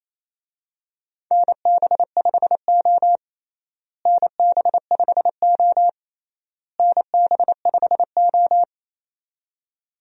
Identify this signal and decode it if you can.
Morse code, heard as a keyed tone, about 28 wpm, 700 Hz. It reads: NB5O NB5O NB5O
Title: USA Call Signs - 4 Characters - Repeated 3x - 28wpm